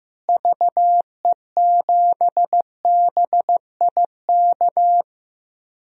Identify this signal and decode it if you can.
Morse code, heard as a keyed tone, about 16 wpm, 700 Hz. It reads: VE7BIK
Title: DXCC-100 Call Signs - 6 Characters - Repeated 3x - 15wpm